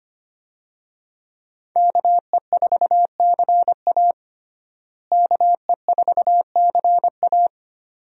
Morse code, heard as a keyed tone, about 25 wpm, 700 Hz. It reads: KE4CA KE4CA
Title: USA Call Signs - 5 Characters - Repeated 2x - 25wpm